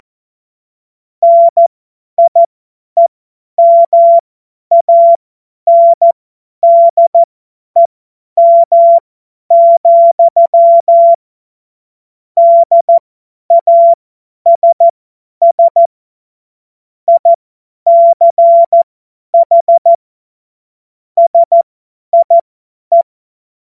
Morse code, heard as a keyed tone, about 14 wpm, spaced as for 10 wpm, 680 Hz. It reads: NIEMANDEM, DASS ICH SIE
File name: Witz-01-Juli-2025-50-F70